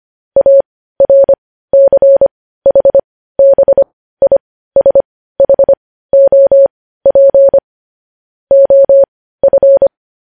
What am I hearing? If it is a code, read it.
Morse code, heard as a keyed tone, about 25 wpm, spaced as for 15 wpm, 550 Hz. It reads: ARCHBISHOP OF
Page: news-headlines-in-morse-code-at-15-wpm